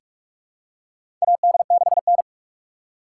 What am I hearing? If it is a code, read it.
Morse code, heard as a keyed tone, about 46 wpm, 700 Hz. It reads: AD6N
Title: USA Call Signs - 4 Characters - 4x Character Spacing - 45wpm